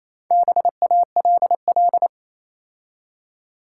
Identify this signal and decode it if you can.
Morse code, heard as a keyed tone, about 27 wpm, 700 Hz. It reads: BALL
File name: Easy Words - Speed-Racing with Hints - 22wpm